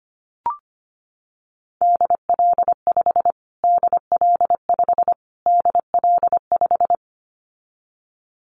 Morse code, heard as a keyed tone, about 25 wpm, 700 Hz. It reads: DL5 DL5 DL5
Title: Callsign Prefixes for Top 25 DXCC - Repeated 3x - 25wpm